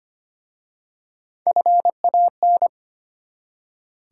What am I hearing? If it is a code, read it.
Morse code, heard as a keyed tone, about 25 wpm, 690 Hz. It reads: FAN
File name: practice / 25wpm / Easy Words with Hints - 25wpm